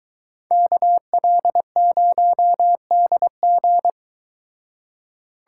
Morse code, heard as a keyed tone, about 22 wpm, 700 Hz. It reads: KL0DG